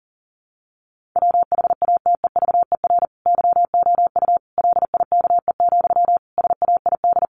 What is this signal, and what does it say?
Morse code, heard as a keyed tone, about 40 wpm, 700 Hz. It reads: WHATEVER YOU LIKE, SAID
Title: Sentences from Top 1100 Words 40wpm